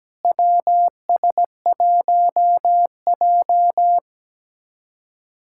Morse code, heard as a keyed tone, about 17 wpm, 700 Hz. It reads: WS1J
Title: USA Callsign DE Callsign - 4 Characters - 17wpm